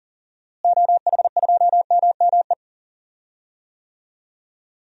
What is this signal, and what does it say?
Morse code, heard as a keyed tone, about 40 wpm, 700 Hz. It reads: OH2MME
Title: DXCC-100 Call Signs - 6 Characters - Repeated 3x - 40wpm